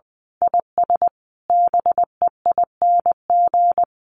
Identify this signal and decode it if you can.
Morse code, heard as a keyed tone, about 20 wpm, 700 Hz. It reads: IS BEING
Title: Sentences from Top 1500 Words 20wpm